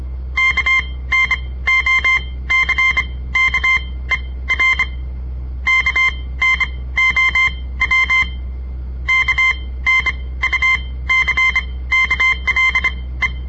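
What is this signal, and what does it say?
Morse code, heard as a keyed tone, about 26 wpm, spaced as for 17 wpm, 2100 Hz. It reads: KNOCKER KNOW KNUCKLE
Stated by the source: On-air Morse code practice broadcast by a southern Montana amateur radio association; all code, no read back.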